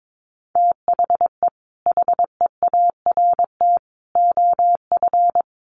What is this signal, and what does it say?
Morse code, heard as a keyed tone, about 22 wpm, 700 Hz. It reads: THE HEART OF
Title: Sets of 3 Words 22wpm